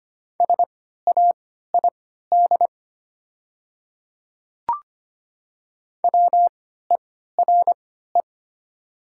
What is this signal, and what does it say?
Morse code, heard as a keyed tone, about 25 wpm, spaced as for 14 wpm, 700 Hz. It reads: SAID WERE
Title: 3x Character Spacing - Top 100 Words - 25wpm